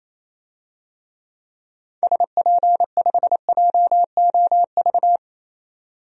Morse code, heard as a keyed tone, about 28 wpm, 700 Hz. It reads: SP5JOV
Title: DXCC-100 Call Signs - 6 Characters - 28wpm